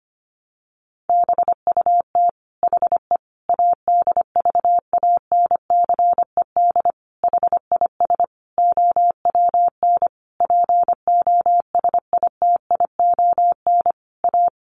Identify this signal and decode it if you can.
Morse code, heard as a keyed tone, about 25 wpm, 700 Hz. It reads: BUT HE ADVANCED HIS OWN POSITION A
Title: Sentences from Top 1200 Words 25wpm